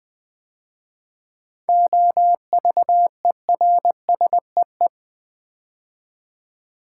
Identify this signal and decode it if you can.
Morse code, heard as a keyed tone, about 20 wpm, 700 Hz. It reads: OVERSEE
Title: 7 Letter Words - First 3 Letters Given - 20wpm